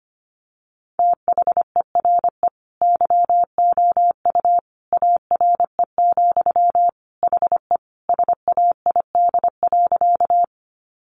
Morse code, heard as a keyed tone, about 25 wpm, 700 Hz. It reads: THERE YOU ARE, HE SAID.
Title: Sentences from Top 100 Words 25wpm